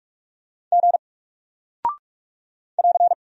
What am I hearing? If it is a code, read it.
Morse code, heard as a keyed tone, about 44 wpm, 700 Hz. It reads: G <AR>
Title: Single Character (Letter:Number:Punctuation:Prosign) Alt 45wpm